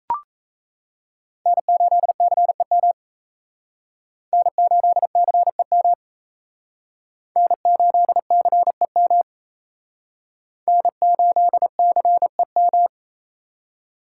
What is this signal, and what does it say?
Morse code, heard as a keyed tone, about 35 wpm, 700 Hz. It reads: N8CEM N8CEM N8CEM N8CEM
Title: Speed Racing - USA 1x3 Call Signs 28wpm